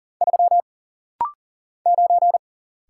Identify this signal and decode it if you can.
Morse code, heard as a keyed tone, about 40 wpm, 700 Hz. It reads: 3 9
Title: Single Letter-Number - Rapid-Fire 40wpm